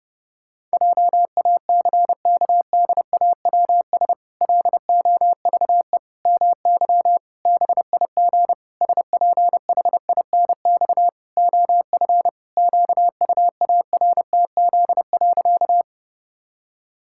Morse code, heard as a keyed tone, about 32 wpm, 700 Hz. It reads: JACKDAWS LOVE MY BIG SPHINX OF QUARTZ.
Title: Pangram Challenge v2 - 30wpm